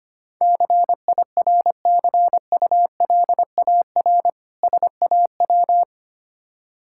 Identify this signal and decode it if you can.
Morse code, heard as a keyed tone, about 25 wpm, 700 Hz. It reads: CIRCULAR SAW
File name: practice / 25wpm / Difficult Words with Hints - 25wpm